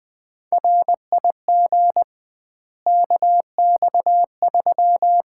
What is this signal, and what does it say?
Morse code, heard as a keyed tone, about 21 wpm, 700 Hz. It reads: RIG KX3
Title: Common QSO Phrases - 2x Word-Spacing - 20wpm